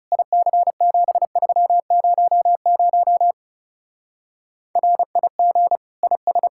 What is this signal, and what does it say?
Morse code, heard as a keyed tone, about 37 wpm, 700 Hz. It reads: IC7300 RIG IS
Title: Speed-Racing Common QSO Phrases - 30wpm